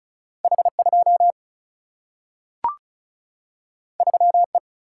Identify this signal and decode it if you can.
Morse code, heard as a keyed tone, about 36 wpm, 700 Hz. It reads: H 2 3 E
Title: Single-Letter Single-Number 35wpm